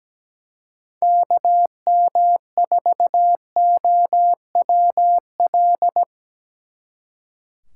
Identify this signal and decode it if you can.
Morse code, heard as a keyed tone, about 19 wpm, 700 Hz. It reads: KM4OWL
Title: Speed Racing - USA Call Signs 17wpm